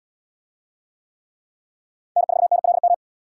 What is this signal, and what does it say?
Morse code, heard as a keyed tone, about 72 wpm, 700 Hz. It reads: N4NLM